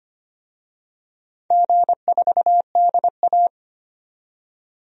Morse code, heard as a keyed tone, about 25 wpm, 700 Hz. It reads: G4DA
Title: DXCC-100 Call Signs - 4 Characters - 25wpm